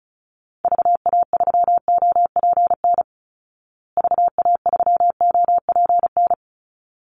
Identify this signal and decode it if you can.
Morse code, heard as a keyed tone, about 34 wpm, 700 Hz. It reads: VA3OPN VA3OPN
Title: Call Signs Repeated 3x - 35wpm